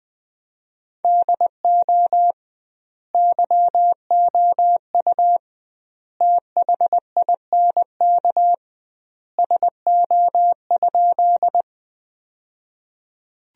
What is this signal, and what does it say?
Morse code, heard as a keyed tone, about 21 wpm, 700 Hz. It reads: DO YOU THINK SO?
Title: 2x Word-Spacing - Sentences from Top 200 Words - 20wpm